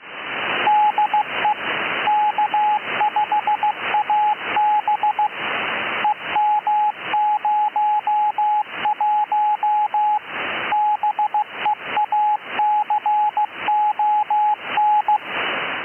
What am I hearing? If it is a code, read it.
Morse code, heard as a keyed tone, about 16 wpm, 840 Hz, with radio noise. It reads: DE K5AB EM01 BEACON
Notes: Relax and take an audio tour of the 10 meter band!